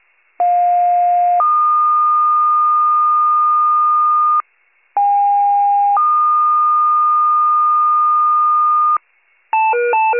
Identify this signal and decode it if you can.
Two-tone page, heard as a one-second tone, then a three-second tone, about 710 then 1220 Hz, about 810 then 1220 Hz. It is heard from 0.4 to 9.0 s.